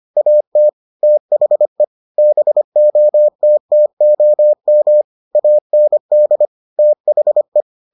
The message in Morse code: AT THE BOTTOM AND THE

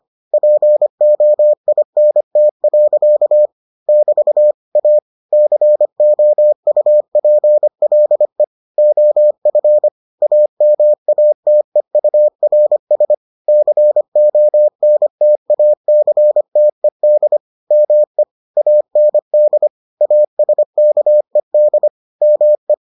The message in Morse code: POINT. <BT> A COUPLE OF AMATEURS CONTACTED ME AND ASKED ME

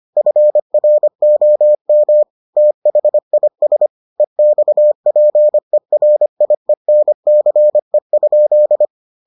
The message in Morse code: FROM THIS EXPERIENCE?

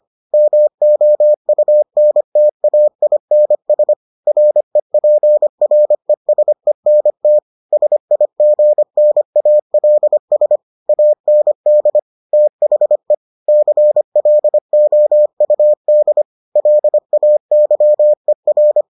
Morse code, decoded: MOUNTAINS REPRESENT SIGNALS AND THE CLOUD LAYER

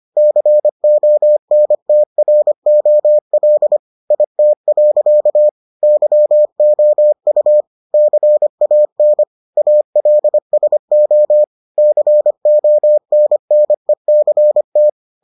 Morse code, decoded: CONTROL IT. YOU CAN ALSO CONNECT

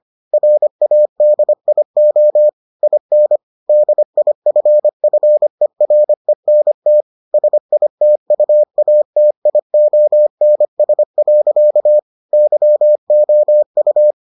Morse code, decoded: RADIO IN DIFFERENT SITUATIONS. YOU

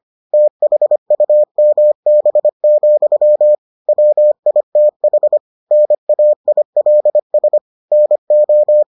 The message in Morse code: THUMB, WITH NAILS NO

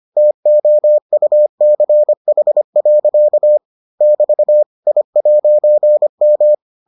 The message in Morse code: TOUCH. <BT> I'M

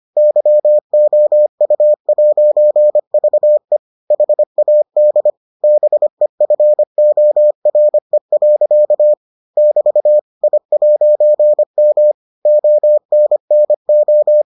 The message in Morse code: YOU'VE HAD BEFORE. <BT> I'M ONNO

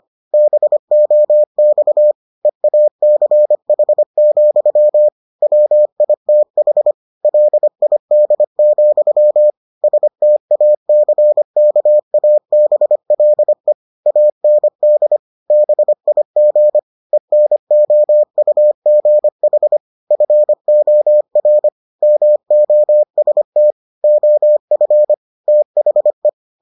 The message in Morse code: BOX EACH, WITH LID, STACKABLE AND BIG ENOUGH FOR MOST OF THE

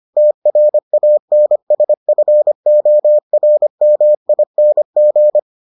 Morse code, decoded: TRANSFORMING

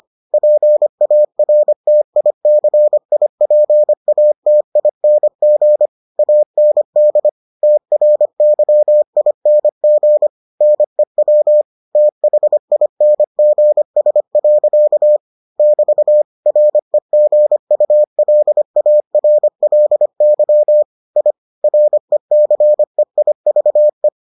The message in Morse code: PARTICIPATING AND TRYING NEW THINGS. <BT> REGULARLY I RECEIVE

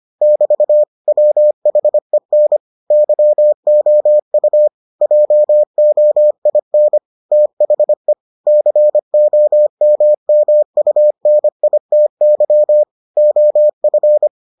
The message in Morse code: <BT> WHEN YOU JOIN THE COMMUNITY OF

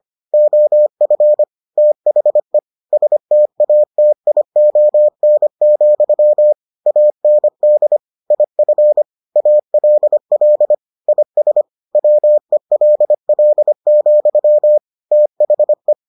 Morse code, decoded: OF THE STATION, AND IF ALL IS WELL, THE

